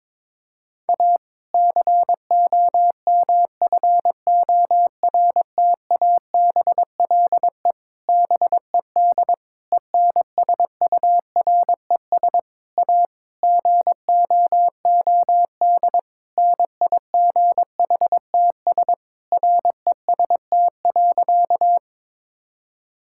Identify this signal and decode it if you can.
Morse code, heard as a keyed tone, about 22 wpm, 700 Hz. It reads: A COMFORTABLE BED ENSURES A GOOD NIGHTS REST.
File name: Difficult Sentences with Hints - 22wpm